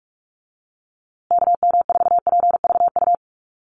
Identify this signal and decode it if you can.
Morse code, heard as a keyed tone, about 44 wpm, 700 Hz. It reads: KM4PVU
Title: USA Call Signs Repeated 3x 45wpm